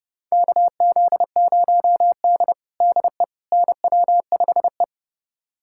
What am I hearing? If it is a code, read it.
Morse code, heard as a keyed tone, about 29 wpm, 700 Hz. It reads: KZ0D DE NW5E
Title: USA Callsign DE Callsign - 4 Characters - 30wpm